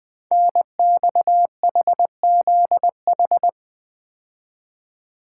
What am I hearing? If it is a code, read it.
Morse code, heard as a keyed tone, about 20 wpm, 700 Hz. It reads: NXHZH
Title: Code Groups - 5 Letters - 20wpm